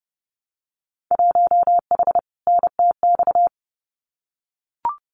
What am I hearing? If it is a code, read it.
Morse code, heard as a keyed tone, about 30 wpm, 700 Hz. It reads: 1H NTX
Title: Winter Field Day Exchanges 30wpm